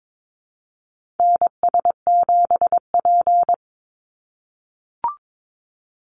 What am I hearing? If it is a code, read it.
Morse code, heard as a keyed tone, about 22 wpm, 700 Hz. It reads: NS7P
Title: NA SOTA Chaser Callsigns 22wpm